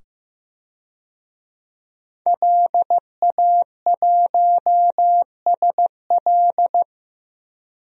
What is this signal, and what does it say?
Morse code, heard as a keyed tone, about 15 wpm, 700 Hz. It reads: LA1SL
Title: DXCC-100 Call Signs - 5 Characters - 15wpm